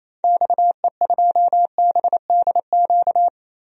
Morse code, heard as a keyed tone, about 28 wpm, 700 Hz. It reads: XE2BDQ
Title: DXCC-100 Call Signs - 6 Characters - Repeated 2x - 28wpm